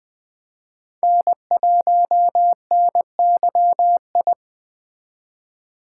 Morse code, heard as a keyed tone, about 20 wpm, 700 Hz. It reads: N1NYI